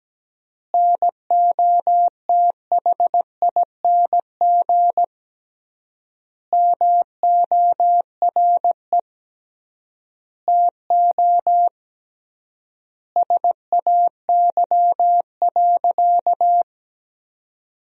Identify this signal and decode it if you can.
Morse code, heard as a keyed tone, about 17 wpm, 700 Hz. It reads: NOTHING MORE TO SAY.
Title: 3x Word-Spacing - Sentences from Top 500 Words - 17wpm